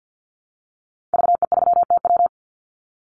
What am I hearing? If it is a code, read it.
Morse code, heard as a keyed tone, about 50 wpm, 700 Hz. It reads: VE3TW
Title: Call Signs - 50wpm